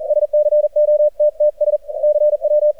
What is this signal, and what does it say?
Morse code, heard as a keyed tone, about 39 wpm, 600 Hz, with radio noise. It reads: SCOTTS CW
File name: archived / CRSnet